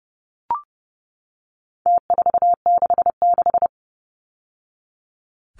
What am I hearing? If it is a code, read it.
Morse code, heard as a keyed tone, about 30 wpm, 700 Hz. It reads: T466
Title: Numbers - Contest Style - 4-Digit 30wpm